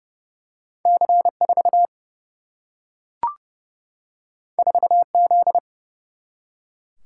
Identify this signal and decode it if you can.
Morse code, heard as a keyed tone, about 30 wpm, 700 Hz. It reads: C 4 4 Z
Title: Single-Letter Single-Number 30wpm